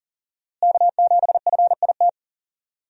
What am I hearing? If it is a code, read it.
Morse code, heard as a keyed tone, about 40 wpm, 700 Hz. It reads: K7FIT
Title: USA Call Signs - 5 Characters - Repeated 3x - 40wpm